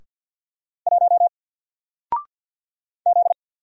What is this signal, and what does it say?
Morse code, heard as a keyed tone, about 50 wpm, 700 Hz. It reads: 1 Z
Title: Single Letter-Number 50wpm